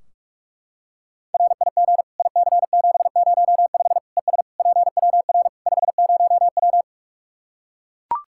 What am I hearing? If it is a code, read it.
Morse code, heard as a keyed tone, about 41 wpm, 700 Hz. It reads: RIG IC705 ES PWR 50W
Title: Practice QSO - Exchange 2 - Info - 45wpm